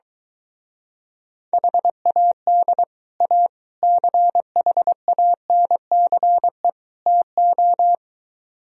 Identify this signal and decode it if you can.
Morse code, heard as a keyed tone, about 23 wpm, 700 Hz. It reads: HAD A CHANCE TO